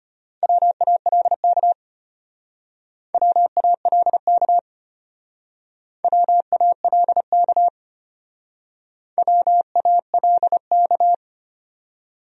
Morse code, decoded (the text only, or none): WALK WALK WALK WALK